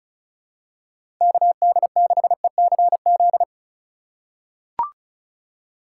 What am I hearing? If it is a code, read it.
Morse code, heard as a keyed tone, about 36 wpm, 690 Hz. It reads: KD6ECZ
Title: USA Call Signs - 6 Characters - 6x Character Spacing - 35wpm